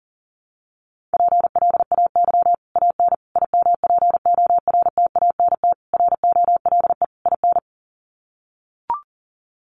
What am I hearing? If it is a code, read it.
Morse code, heard as a keyed tone, about 40 wpm, 700 Hz. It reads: PLAY AN IMPORTANT ROLE IN
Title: Sets of 5 Words Encore 40wpm